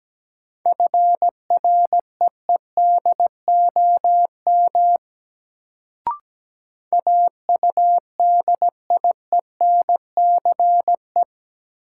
Morse code, heard as a keyed tone, about 17 wpm, 700 Hz. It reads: FREEDOM AUDIENCE
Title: Sentences from Top 800 Words - New Word Drill - 17wpm